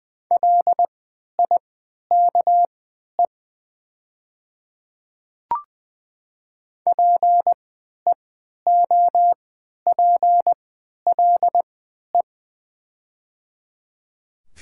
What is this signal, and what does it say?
Morse code, heard as a keyed tone, about 20 wpm, spaced as for 11 wpm, 700 Hz. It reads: LIKE PEOPLE
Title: 3x Character Spacing - Top 100 Words - 20wpm